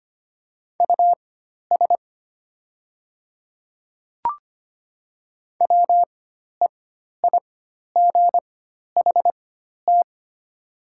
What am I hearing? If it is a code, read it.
Morse code, heard as a keyed tone, about 25 wpm, spaced as for 12 wpm, 700 Hz. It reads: US WEIGHT
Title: 4x Character Spacing - Top 500 Words - 25wpm